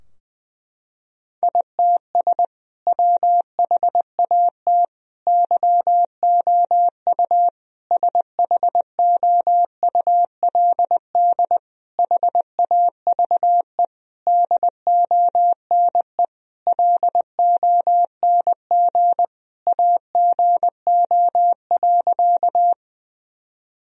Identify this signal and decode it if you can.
Morse code, heard as a keyed tone, about 20 wpm, 700 Hz. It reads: ITS WHAT YOU SHOULD HAVE DONE LONG AGO.
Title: Sentences from Top 1000 Words - Review Part 3 - 20wpm